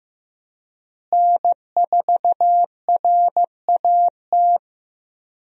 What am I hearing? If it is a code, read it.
Morse code, heard as a keyed tone, about 15 wpm, 700 Hz. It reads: N4RAT
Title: USA Call Signs - 5 Characters - Repeated 2x - 15wpm